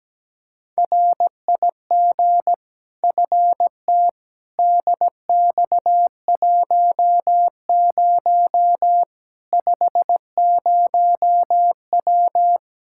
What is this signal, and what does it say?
Morse code, heard as a keyed tone, about 17 wpm, 700 Hz. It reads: RIG FT DX10 50W
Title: K1USN SST QSO Practice 17wpm